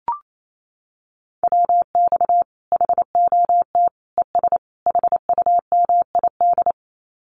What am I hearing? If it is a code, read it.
Morse code, heard as a keyed tone, about 28 wpm, 700 Hz. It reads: WX HOT ES HUMID
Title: Common QSO Phrases 28wpm